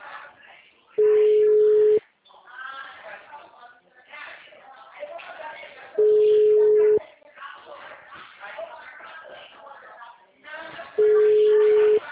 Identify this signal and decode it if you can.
Ringback tone (Europe, 425 Hz), with speech.
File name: out- -2013- - - .46880